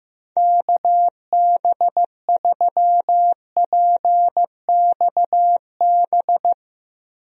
Morse code, heard as a keyed tone, about 15 wpm, 700 Hz. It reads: KB3PXB
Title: USA Call Signs - 6 Characters - Repeated 2x - 15wpm